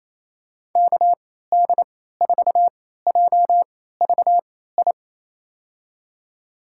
Morse code, heard as a keyed tone, about 27 wpm, spaced as for 16 wpm, 700 Hz. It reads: KD4JVI
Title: USA Call Signs - 6 Characters - 3x Character Spacing - 28wpm